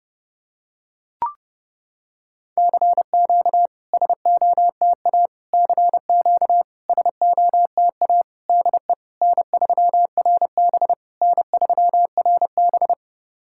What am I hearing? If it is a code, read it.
Morse code, heard as a keyed tone, about 32 wpm, 700 Hz. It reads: CQ SOTA CQ SOTA DE N3RB N3RB
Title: SOTA QSO Practice 30wpm